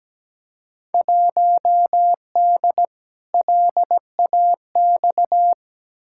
Morse code, heard as a keyed tone, about 17 wpm, 700 Hz. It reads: 1D LAX
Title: Field Day Exchanges - 17wpm